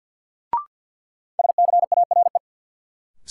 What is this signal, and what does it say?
Morse code, heard as a keyed tone, about 50 wpm, 700 Hz. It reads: SCARE
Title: Sentences from Top 1000 Words - Review Part 5 - New Word Drill - 50wpm